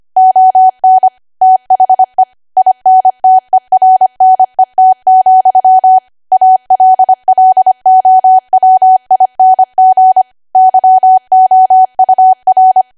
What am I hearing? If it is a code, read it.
Morse code, heard as a keyed tone, about 25 wpm, 750 Hz. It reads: ON THE INTERNET, ALLOWING YOUR
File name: Archive / 25 WPM / 25WPM